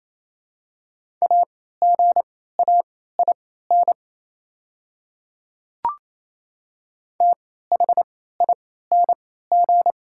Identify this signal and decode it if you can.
Morse code, heard as a keyed tone, about 28 wpm, spaced as for 16 wpm, 700 Hz. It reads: AGAIN THING
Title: 3x Character Spacing - Top 200 Words - 28wpm